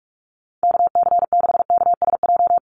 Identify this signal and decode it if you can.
Morse code, heard as a keyed tone, about 44 wpm, 700 Hz. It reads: KC6KSJ
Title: USA Call Signs Repeated 3x 45wpm